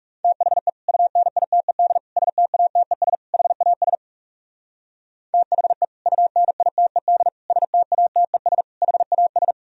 Morse code, heard as a keyed tone, about 35 wpm, 700 Hz. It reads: THE UNITED STATES HAS THE UNITED STATES HAS
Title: Speed-Racing - Sets of 4 Words 30wpm